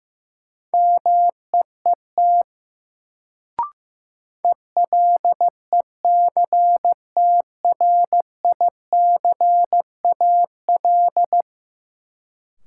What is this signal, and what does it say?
Morse code, heard as a keyed tone, about 15 wpm, 700 Hz. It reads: MEET ELECTRICAL
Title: Sentences from Top 1000 Words - Review Part 6 - New Word Drill - 15wpm